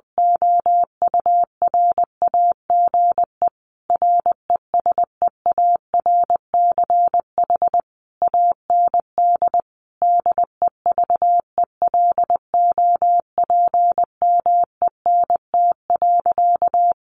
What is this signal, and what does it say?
Morse code, heard as a keyed tone, about 20 wpm, 700 Hz. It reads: OURAGE RESEARCH AND DEVELOPMENT.
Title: Sentences from Top 1400 Words 20wpm